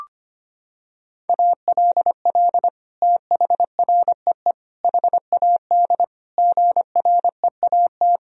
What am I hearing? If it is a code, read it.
Morse code, heard as a keyed tone, about 25 wpm, 700 Hz. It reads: ALL THREE HAD GREAT
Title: Sentences from Top 1000 Words - Review Part 2 - 25wpm